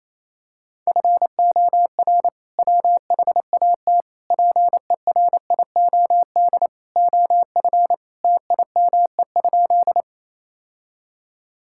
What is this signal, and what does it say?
Morse code, heard as a keyed tone, about 28 wpm, 700 Hz. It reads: FOR WHAT PERIOD OF TIME?
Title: Sentences from Top 1000 Words - Review Part 5 - 28wpm